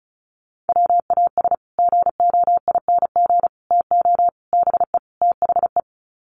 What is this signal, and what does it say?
Morse code, heard as a keyed tone, about 34 wpm, 700 Hz. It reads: WAS GOING TO BE THE
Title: Sets of 5 Words Encore2 35wpm